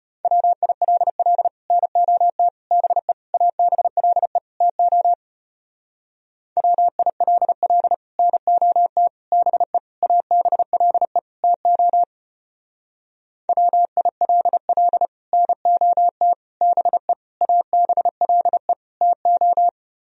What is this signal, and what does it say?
Morse code, heard as a keyed tone, about 33 wpm, 700 Hz. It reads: WILL NOT BE ABLE TO WILL NOT BE ABLE TO WILL NOT BE ABLE TO